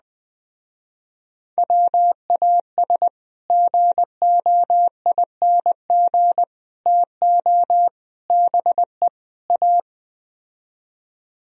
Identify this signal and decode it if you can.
Morse code, heard as a keyed tone, about 20 wpm, 700 Hz. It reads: WAS GOING TO BE A